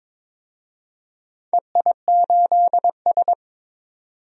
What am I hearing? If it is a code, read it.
Morse code, heard as a keyed tone, about 21 wpm, 700 Hz. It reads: EI8S
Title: DXCC-100 Call Signs - 4 Characters - 22wpm